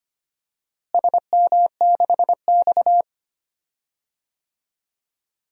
Morse code, heard as a keyed tone, about 25 wpm, 700 Hz. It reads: SM6X
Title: DXCC-100 Call Signs - 4 Characters - Repeated 2x - 25wpm